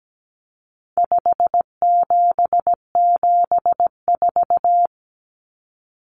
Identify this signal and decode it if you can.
Morse code, heard as a keyed tone, about 17 wpm, 700 Hz. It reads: 5774
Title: Common Cut Numbers - 4-Digit 17wpm